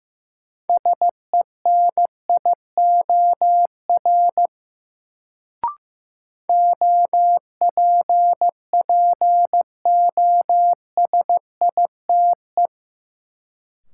Morse code, heard as a keyed tone, about 15 wpm, 700 Hz. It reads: SENIOR OPPOSITE
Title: Sets of 5 Words Encore - New Word Drill - 15wpm